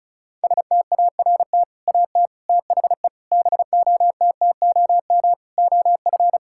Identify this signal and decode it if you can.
Morse code, heard as a keyed tone, about 34 wpm, 700 Hz. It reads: START AT THE BOTTOM OF
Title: Sentences from Top 1000 Words - Review Part 6 - 35wpm